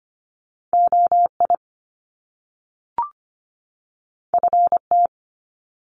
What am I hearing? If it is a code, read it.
Morse code, heard as a keyed tone, about 25 wpm, 700 Hz. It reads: OI FT
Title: Top Two Letter Combinations in Words 25wpm